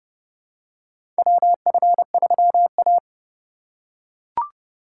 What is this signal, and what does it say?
Morse code, heard as a keyed tone, about 30 wpm, 700 Hz. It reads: WF3A
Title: USA Call Signs - 4x Character Spacing - 30wpm